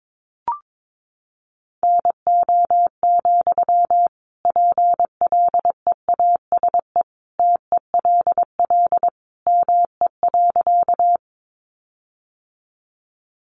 Morse code, heard as a keyed tone, about 22 wpm, 700 Hz. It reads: NO, PLEASE TELL ME.
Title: Sentences from Top 1500 Words 22wpm